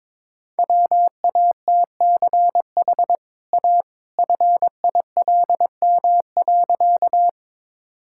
Morse code, heard as a keyed tone, about 22 wpm, 690 Hz. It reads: WATCH A FILM.
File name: practice / 22wpm / Easy Sentences with Hints - 22wpm